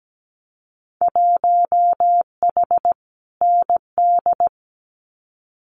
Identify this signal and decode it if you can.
Morse code, heard as a keyed tone, about 17 wpm, 700 Hz. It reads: 1H ND
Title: Winter Field Day Exchanges 17wpm